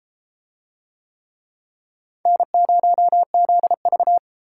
Morse code, heard as a keyed tone, about 32 wpm, 700 Hz. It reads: N0ZV